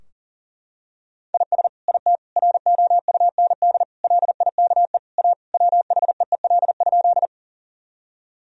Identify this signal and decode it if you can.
Morse code, heard as a keyed tone, about 37 wpm, 700 Hz. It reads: IS IT ROUND LIKE A WHEEL?
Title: Sentences from Top 1000 Words - Review Part 3 - 40wpm